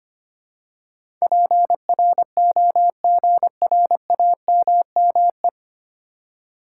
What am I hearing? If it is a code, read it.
Morse code, heard as a keyed tone, about 25 wpm, 700 Hz. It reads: PROGRAMME
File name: Nine Letter Words - 25wpm